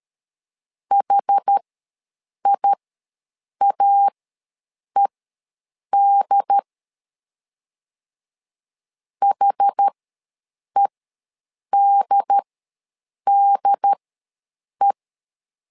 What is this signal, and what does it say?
Morse code, heard as a keyed tone, about 13 wpm, spaced as for 7 wpm, 800 Hz. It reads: HIAED HEDDE
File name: LEZIONI / 2 UHDN / 2 UHD ws 12wpm 1722